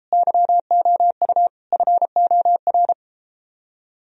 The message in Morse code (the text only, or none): YOU FOR